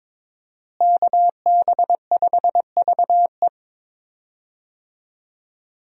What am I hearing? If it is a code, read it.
Morse code, heard as a keyed tone, about 22 wpm, 700 Hz. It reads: KB5VE
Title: USA Call Signs - 5 Characters - Repeated 3x - 22wpm